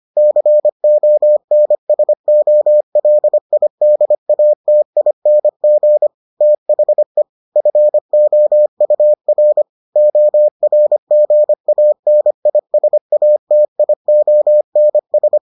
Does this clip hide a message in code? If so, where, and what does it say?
Yes: Morse code from 0.2 to 15.4 s, heard as a keyed tone, about 25 wpm, 600 Hz. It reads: CONSOLIDATING THE FOUR ORGANISATIONS